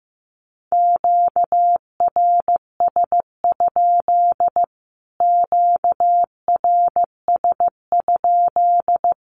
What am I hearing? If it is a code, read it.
Morse code, heard as a keyed tone, about 15 wpm, 700 Hz. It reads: QRS? QRS?
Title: Q-Codes Repeated 3x 15wpm